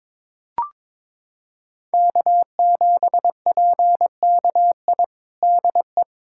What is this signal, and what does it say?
Morse code, heard as a keyed tone, about 22 wpm, 700 Hz. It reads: K7PKI DE
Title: USA Callsign DE Callsign - 5 Characters - 22wpm